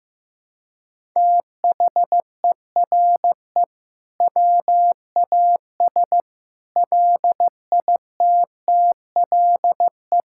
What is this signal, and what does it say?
Morse code, heard as a keyed tone, about 15 wpm, 700 Hz. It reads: THERE WAS LITTLE
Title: Sentences from Top 1000 Words - Review Part 5 - 15wpm